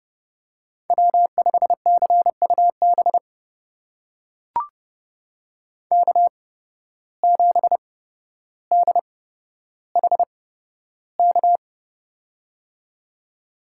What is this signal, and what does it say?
Morse code, heard as a keyed tone, about 30 wpm, spaced as for 13 wpm, 700 Hz. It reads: W5CUB K7DHK
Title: USA Call Signs - 5 Characters - 8x Character Spacing - 30wpm